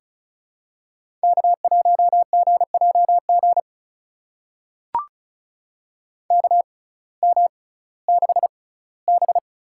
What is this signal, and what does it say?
Morse code, heard as a keyed tone, about 36 wpm, spaced as for 19 wpm, 700 Hz. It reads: K1GJG KM6B
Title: USA Call Signs - 5 Characters - 6x Character Spacing - 35wpm